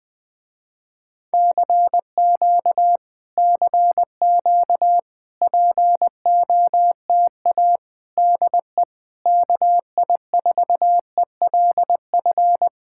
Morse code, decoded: CQ CQ POTA DE KI4ELF

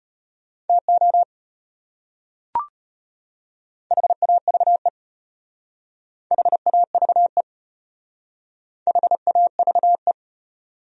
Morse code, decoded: TO HAVE HAVE HAVE

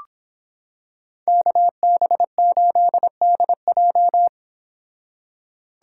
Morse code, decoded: KB8DJ

